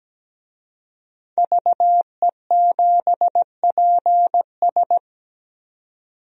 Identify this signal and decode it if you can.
Morse code, heard as a keyed tone, about 17 wpm, 700 Hz. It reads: VE7PS
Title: DXCC-100 Call Signs - 5 Characters - 17wpm